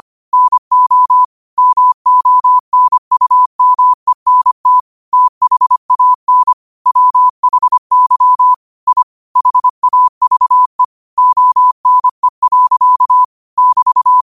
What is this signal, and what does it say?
Morse code, heard as a keyed tone, about 25 wpm, 1000 Hz. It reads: NO MONUMENT THAN WHY I HAVE ONE. <BT>
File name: qotdfiles / 25wpm